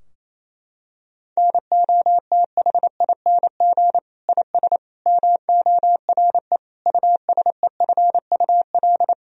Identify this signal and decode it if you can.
Morse code, heard as a keyed tone, about 27 wpm, 700 Hz. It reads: NOTHING IS MORE USEFUL
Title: Sentences from Top 1000 Words - Review Part 4 - 28wpm